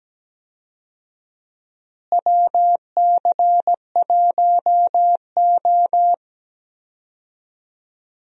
Morse code, heard as a keyed tone, about 17 wpm, 700 Hz. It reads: WC1O
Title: USA Call Signs - 4 Characters - Repeated 2x - 17wpm